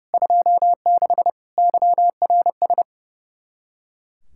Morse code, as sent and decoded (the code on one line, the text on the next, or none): ..--- -.... / -.-- .-. ...
26 YRS